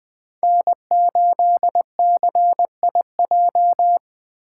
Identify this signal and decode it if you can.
Morse code, heard as a keyed tone, about 20 wpm, 700 Hz. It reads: N8CIJ
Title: Speed Racing - USA 1x3 Call Signs 17wpm